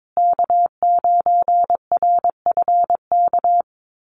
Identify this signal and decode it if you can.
Morse code, heard as a keyed tone, about 22 wpm, 700 Hz. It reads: K9RFK
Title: USA Call Signs Repeated 3x 22wpm